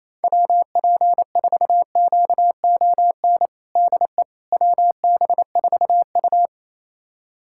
Morse code, heard as a keyed tone, about 28 wpm, 700 Hz. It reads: WP4QON DE WB4U
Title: USA Callsign DE Callsign 28wpm